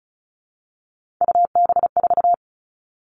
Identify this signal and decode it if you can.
Morse code, heard as a keyed tone, about 34 wpm, 700 Hz. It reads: UB4
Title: All Cut Numbers - 3-Digit 35wpm